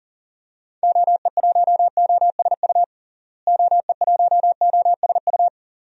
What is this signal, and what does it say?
Morse code, heard as a keyed tone, about 40 wpm, 700 Hz. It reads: OE1OSU OE1OSU
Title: DXCC-100 Call Signs - 6 Characters - Repeated 2x - 40wpm